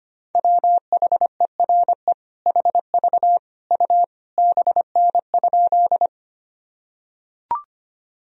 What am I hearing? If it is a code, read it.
Morse code, heard as a keyed tone, about 25 wpm, 700 Hz. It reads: WHERE HV U BN?
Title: Sentences from Top 200 Words and Abbreviations 25wpm